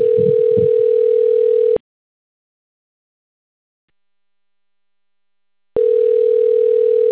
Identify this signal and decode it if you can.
Ringback tone (North America), 440 + 480 Hz.